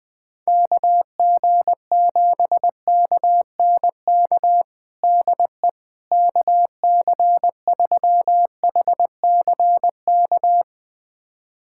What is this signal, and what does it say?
Morse code, heard as a keyed tone, about 21 wpm, 700 Hz. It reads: KG7KNK DE KC3HCK
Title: USA Callsign DE Callsign - 6 Characters - 20wpm